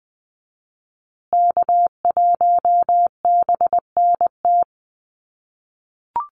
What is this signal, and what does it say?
Morse code, heard as a keyed tone, about 20 wpm, 700 Hz. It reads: K1BNT
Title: USA Call Signs Repeated 3x 20wpm